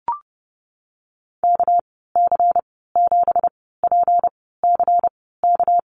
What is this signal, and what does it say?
Morse code, heard as a keyed tone, about 30 wpm, spaced as for 17 wpm, 700 Hz. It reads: KC7PCK
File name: Lesson 37g - Letter x - Call Signs 30wpm